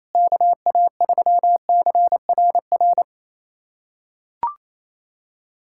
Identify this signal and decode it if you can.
Morse code, heard as a keyed tone, about 28 wpm, 700 Hz. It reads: KA3CRR
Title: USA Callsign DE Callsign - 6 Characters - 28wpm